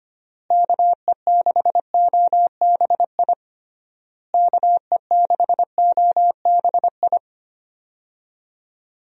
Morse code, heard as a keyed tone, about 25 wpm, 700 Hz. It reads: KE6OBI KE6OBI
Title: USA Call Signs - 6 Characters - Repeated 3x - 25wpm